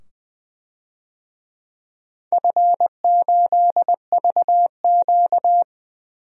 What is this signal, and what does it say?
Morse code, heard as a keyed tone, about 20 wpm, 700 Hz. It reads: F8VQ
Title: DXCC-100 Call Signs - 4 Characters - 20wpm